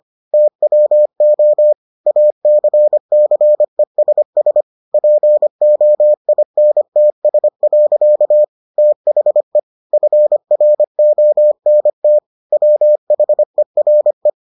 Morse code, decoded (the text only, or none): TWO ACCESS POINTS. THE FRONT WHERE